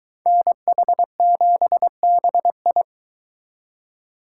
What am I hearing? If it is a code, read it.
Morse code, heard as a keyed tone, about 23 wpm, 700 Hz. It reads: NH7BI